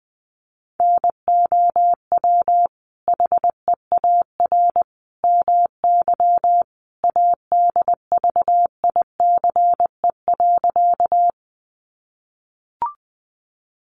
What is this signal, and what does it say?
Morse code, heard as a keyed tone, about 19 wpm, 700 Hz. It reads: NOW HEAR MY ADVICE.
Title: Sentences from Top 1100 Words 20wpm